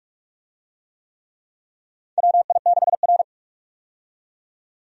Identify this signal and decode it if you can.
Morse code, heard as a keyed tone, about 46 wpm, 700 Hz. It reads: WI6R
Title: USA Call Signs - 4 Characters - 45wpm